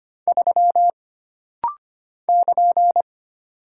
Morse code, heard as a keyed tone, about 25 wpm, 700 Hz. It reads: 3 <KN>
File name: Single Character (Letter:Number:Punctuation:Prosign) - Rapid-Fire - 25wpm